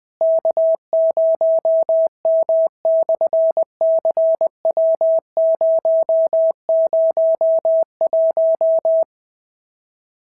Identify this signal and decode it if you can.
Morse code, heard as a keyed tone, about 20 wpm, 690 Hz. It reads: K0M/CW001
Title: SOTA QSO Practice 20wpm